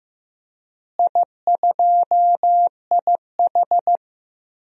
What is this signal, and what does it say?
Morse code, heard as a keyed tone, about 15 wpm, 700 Hz. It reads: I2IH
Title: DXCC-100 Call Signs - 4 Characters - Repeated 2x - 15wpm